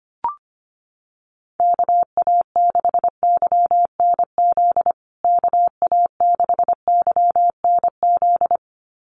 Morse code, heard as a keyed tone, about 25 wpm, 700 Hz. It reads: KA6YNZ KA6YNZ
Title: USA Call Signs Repeated 2x 25wpm